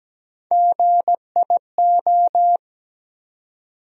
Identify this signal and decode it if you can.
Morse code, heard as a keyed tone, about 17 wpm, 700 Hz. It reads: GIO
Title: USA Call Sign Postfixes - 3 Letters 17wpm